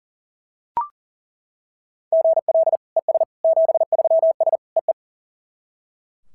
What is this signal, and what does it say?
Morse code, heard as a keyed tone, about 40 wpm, 690 Hz. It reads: GL ES 73S EE
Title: SOTA QSO Practice 40wpm